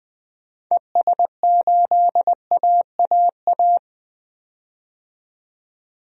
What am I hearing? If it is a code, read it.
Morse code, heard as a keyed tone, about 20 wpm, 700 Hz. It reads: ES8AAA
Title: DXCC-100 Call Signs - 6 Characters - 20wpm